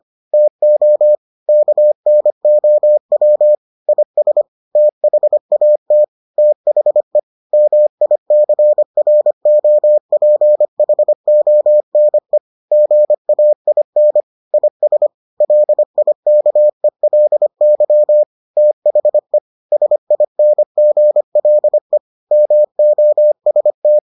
Morse code, heard as a keyed tone, about 25 wpm, 600 Hz. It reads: TO KNOW IS THAT THE MICROPHONE GAIN IS LIKELY THE SINGLE MOST